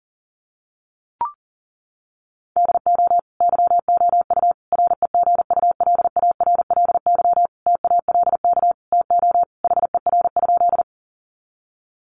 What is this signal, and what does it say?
Morse code, heard as a keyed tone, about 40 wpm, 700 Hz. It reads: DO YOU REGULARLY TALK TO HER?
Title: Sentences from Top 1400 Words 40wpm